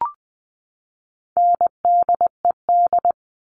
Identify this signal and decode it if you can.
Morse code, heard as a keyed tone, about 20 wpm, 700 Hz. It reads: NDED
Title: All Cut Numbers - 4-Digit 20wpm